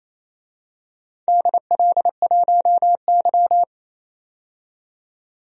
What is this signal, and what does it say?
Morse code, heard as a keyed tone, about 28 wpm, 700 Hz. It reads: DL1Y
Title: DXCC-100 Call Signs - 4 Characters - 28wpm